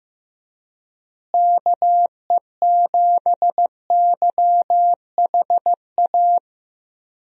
Morse code, heard as a keyed tone, about 15 wpm, 700 Hz. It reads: KE7YHA
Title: USA Call Signs - 6 Characters - Repeated 3x - 15wpm